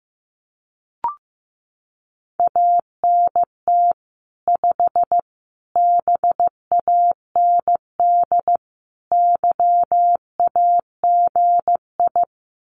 Morse code, heard as a keyed tone, about 15 wpm, 700 Hz. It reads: ANT 5 BAND YAGI
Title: Common QSO Phrases 15wpm